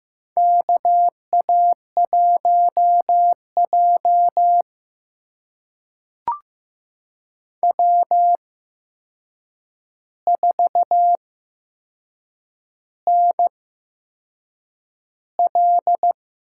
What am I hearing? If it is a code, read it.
Morse code, heard as a keyed tone, about 15 wpm, spaced as for 6 wpm, 700 Hz. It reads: KA1J W4NL
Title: USA Call Signs - 4 Characters - 8x Character Spacing - 15wpm